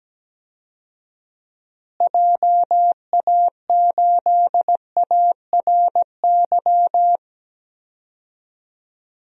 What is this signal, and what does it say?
Morse code, heard as a keyed tone, about 17 wpm, 700 Hz. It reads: JA8ARY
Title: DXCC-100 Call Signs - 6 Characters - Repeated 2x - 17wpm